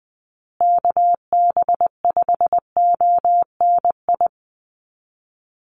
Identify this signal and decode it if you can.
Morse code, heard as a keyed tone, about 20 wpm, 700 Hz. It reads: KB5ONI
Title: USA Call Signs Repeated 3x 20wpm